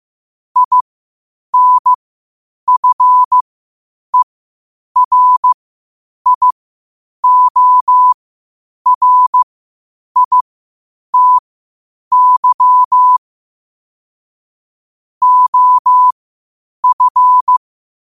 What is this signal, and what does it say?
Morse code, heard as a keyed tone, about 15 wpm, spaced as for 8 wpm, 1000 Hz. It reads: INFERIORITY OF
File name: qotdfiles / 8wpm